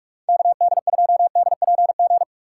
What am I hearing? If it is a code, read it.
Morse code, heard as a keyed tone, about 46 wpm, 700 Hz. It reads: KD2DPG
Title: USA Call Signs - 8x Character Spacing - 45wpm